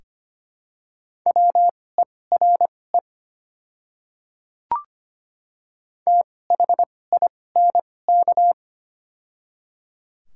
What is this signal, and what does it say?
Morse code, heard as a keyed tone, about 25 wpm, spaced as for 18 wpm, 700 Hz. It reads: WERE THINK
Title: 2x Character Spacing - Top 200 Words - 25wpm